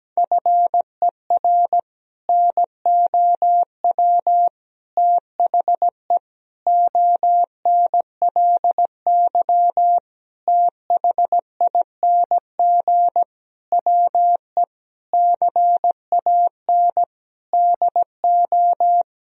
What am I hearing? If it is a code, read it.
Morse code, heard as a keyed tone, about 17 wpm, 700 Hz. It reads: FER NOW THE ONLY THING WE CAN DO
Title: Sentences from Top 200 Words and Abbreviations 17wpm